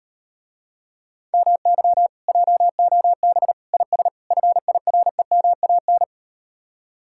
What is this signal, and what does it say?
Morse code, heard as a keyed tone, about 38 wpm, 700 Hz. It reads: MY JOB IS FIREMAN